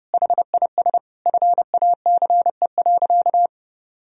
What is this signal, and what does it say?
Morse code, heard as a keyed tone, about 32 wpm, 700 Hz. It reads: HIS FACE.
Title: Sentences from Top 1000 Words - Review Part 2 - 30wpm